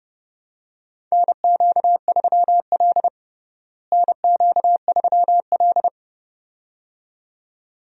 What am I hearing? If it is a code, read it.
Morse code, heard as a keyed tone, about 30 wpm, 700 Hz. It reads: NQ3L NQ3L
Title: USA Call Signs - 4 Characters - Repeated 2x - 30wpm